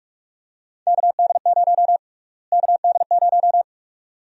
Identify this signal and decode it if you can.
Morse code, heard as a keyed tone, about 46 wpm, 700 Hz. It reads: KD0 KD0
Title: USA Call Sign Prefixes - Repeated 2x - 45wpm